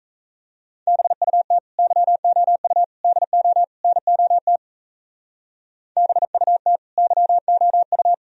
Morse code, heard as a keyed tone, about 36 wpm, 700 Hz. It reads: BUT YOU DO NOT BUT YOU